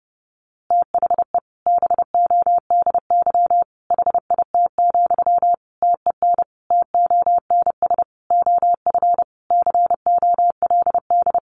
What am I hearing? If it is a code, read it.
Morse code, heard as a keyed tone, about 30 wpm, 700 Hz. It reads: THE BODY HIT, TEN TONS OF COLD
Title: A Sound Of Thunder 30wpm